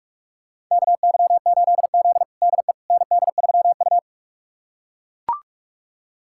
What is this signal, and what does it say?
Morse code, heard as a keyed tone, about 42 wpm, 700 Hz. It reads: KY8Z DE ND3U
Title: USA Callsign DE Callsign - 4 Characters - 45wpm